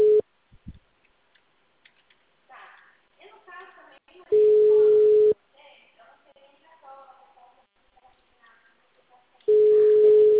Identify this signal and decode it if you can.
Ringback tone (Europe, 425 Hz), with speech.